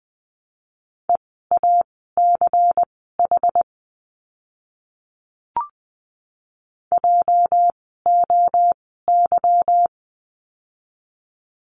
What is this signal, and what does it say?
Morse code, heard as a keyed tone, about 20 wpm, spaced as for 14 wpm, 700 Hz. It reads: EACH JOY
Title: 206 Lesson 33f - Letter j - Words 20wpm